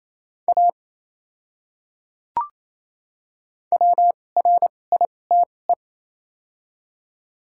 Morse code, heard as a keyed tone, about 28 wpm, spaced as for 20 wpm, 700 Hz. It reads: A WRITE
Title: 2x Character Spacing - Top 100 Words - 28wpm